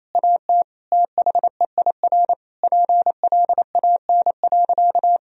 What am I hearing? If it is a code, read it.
Morse code, heard as a keyed tone, about 28 wpm, 690 Hz. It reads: AT THEIR PLAN.
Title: Sentences from Top 1000 Words - Review Part 2 - 28wpm